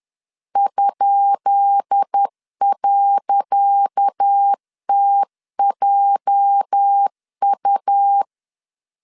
Morse code, heard as a keyed tone, about 10 wpm, 800 Hz. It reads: ?.TJU
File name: LEZIONI / 10 SIGNS / 10C DOT 10wpm 2940